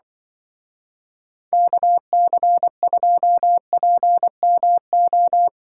Morse code, heard as a keyed tone, about 23 wpm, 700 Hz. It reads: KC2PMO
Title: Speed Racing - USA 2x3 Call Signs 20wpm